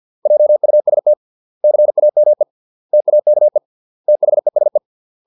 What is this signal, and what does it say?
Morse code, heard as a keyed tone, about 50 wpm, 600 Hz. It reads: JUST CAME TAKE THESE
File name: 632 words / 632 Words 50WPM